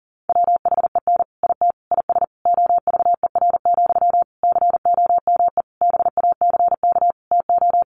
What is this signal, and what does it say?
Morse code, heard as a keyed tone, about 40 wpm, 700 Hz. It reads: WHEN IT IS OVER, COME BACK TO
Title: Sentences from Top 1200 Words 40wpm